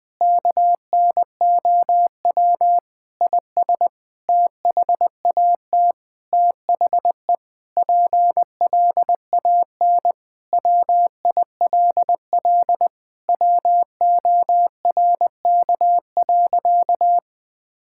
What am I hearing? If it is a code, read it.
Morse code, heard as a keyed tone, about 20 wpm, 700 Hz. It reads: KNOW IS THAT THE PLAN WILL WORK.
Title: Sentences with 7-Word Phrases - 20wpm